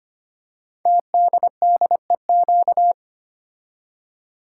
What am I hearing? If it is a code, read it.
Morse code, heard as a keyed tone, about 25 wpm, 700 Hz. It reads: TDDEQ
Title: Code Groups - 5 Letters - 25wpm